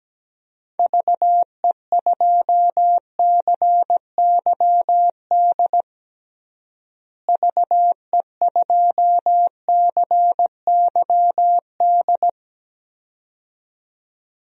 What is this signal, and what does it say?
Morse code, heard as a keyed tone, about 17 wpm, 700 Hz. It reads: VE2CYD VE2CYD
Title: DXCC-100 Call Signs - 6 Characters - Repeated 2x - 17wpm